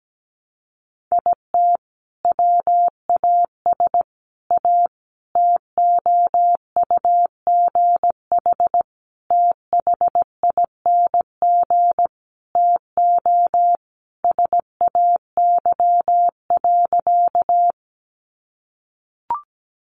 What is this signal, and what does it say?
Morse code, heard as a keyed tone, about 17 wpm, 700 Hz. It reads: IT WAS A TOUGH THING TO SAY.
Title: Sentences from Top 1200 Words 17wpm